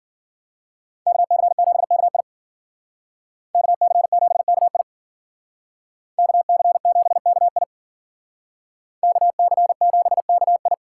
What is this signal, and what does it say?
Morse code, heard as a keyed tone, about 50 wpm, 700 Hz. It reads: KC7KI KC7KI KC7KI KC7KI
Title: Speed Racing - USA Call Signs 40wpm